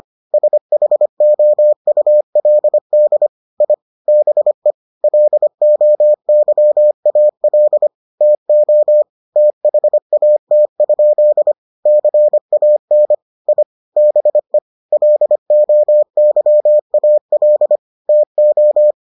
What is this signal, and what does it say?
Morse code, heard as a keyed tone, about 25 wpm, 600 Hz. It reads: SHOULD I BE LOYAL TO THAT? CAN I BE LOYAL TO